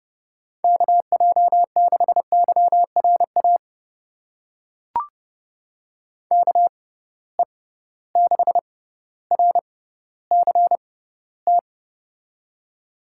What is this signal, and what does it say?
Morse code, heard as a keyed tone, about 30 wpm, spaced as for 15 wpm, 700 Hz. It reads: KJ6YRA KE6RCT
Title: USA Call Signs - 6 Characters - 6x Character Spacing - 30wpm